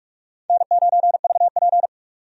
Morse code, heard as a keyed tone, about 46 wpm, 700 Hz. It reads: N9VP
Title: USA Call Signs - 4 Characters - 45wpm